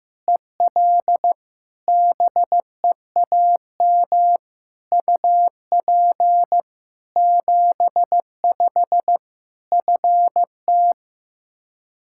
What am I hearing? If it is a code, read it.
Morse code, heard as a keyed tone, about 15 wpm, 700 Hz. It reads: EL BEAM UP 75 FT
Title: K1USN SST QSO Practice 15wpm